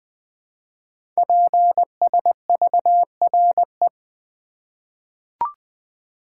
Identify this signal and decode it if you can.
Morse code, heard as a keyed tone, about 21 wpm, 700 Hz. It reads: PSVRE
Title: Code Groups - 5 Letter:Numbers - 20wpm